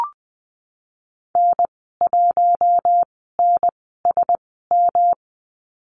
Morse code, heard as a keyed tone, about 20 wpm, spaced as for 15 wpm, 700 Hz. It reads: N1NSM
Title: 147 Lesson 24g - Letter y - Call Signs 20wpm